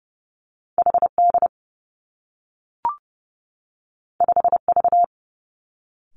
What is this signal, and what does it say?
Morse code, heard as a keyed tone, about 30 wpm, 700 Hz. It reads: H D 5 V
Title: Two Challenging Letter-Number Combinations 30wpm